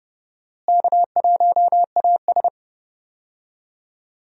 Morse code, heard as a keyed tone, about 30 wpm, 700 Hz. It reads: K1AS
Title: USA Call Signs - 4 Characters - Repeated 3x - 30wpm